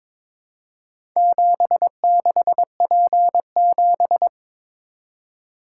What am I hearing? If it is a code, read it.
Morse code, heard as a keyed tone, about 22 wpm, 700 Hz. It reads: 76P7
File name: practice / 22wpm / Four Letter-Number Combinations - 22wpm